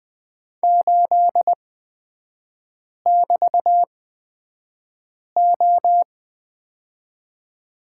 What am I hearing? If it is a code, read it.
Morse code, heard as a keyed tone, about 20 wpm, spaced as for 5 wpm, 700 Hz. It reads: 8<BT>O
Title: 224 Lesson 36e - BT - Three Characters 20wpm